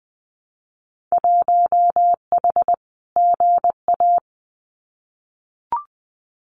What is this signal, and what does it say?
Morse code, heard as a keyed tone, about 20 wpm, 700 Hz. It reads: 1H GA
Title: Winter Field Day Exchanges 20wpm